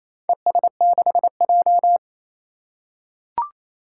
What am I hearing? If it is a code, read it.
Morse code, heard as a keyed tone, about 28 wpm, 700 Hz. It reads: ES6J
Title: Four Challenging Letter-Number Combinations 28wpm